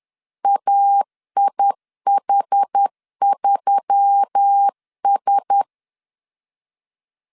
Morse code, heard as a keyed tone, about 10 wpm, 800 Hz. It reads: AIH3S